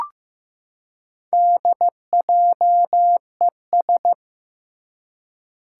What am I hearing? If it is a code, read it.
Morse code, heard as a keyed tone, about 15 wpm, 700 Hz. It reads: DJES
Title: Four Challenging Letter-Number Combinations 15wpm